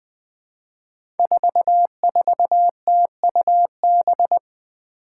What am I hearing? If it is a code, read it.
Morse code, heard as a keyed tone, about 20 wpm, 700 Hz. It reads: 44TUB
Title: Five Challenging Letter-Number Combinations 20wpm